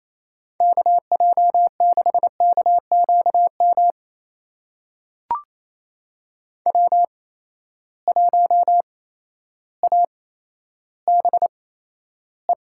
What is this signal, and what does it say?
Morse code, heard as a keyed tone, about 28 wpm, spaced as for 13 wpm, 700 Hz. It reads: KJ6KQM W1ABE
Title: USA Call Signs - 8x Character Spacing - 28wpm